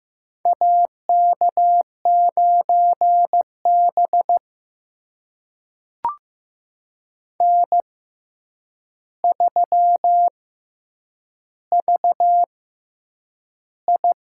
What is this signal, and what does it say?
Morse code, heard as a keyed tone, about 15 wpm, spaced as for 8 wpm, 700 Hz. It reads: AK9B N3VI
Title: USA Call Signs - 4 Characters - 6x Character Spacing - 15wpm